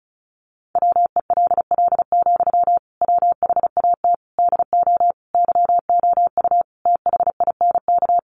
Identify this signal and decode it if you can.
Morse code, heard as a keyed tone, about 34 wpm, 700 Hz. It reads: WELL, WHAT DO YOU THINK
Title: Sentences from Top 200 Words 35wpm